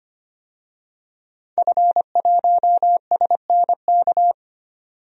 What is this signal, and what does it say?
Morse code, heard as a keyed tone, about 25 wpm, 700 Hz. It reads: F1SNK
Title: DXCC-100 Call Signs - 5 Characters - 25wpm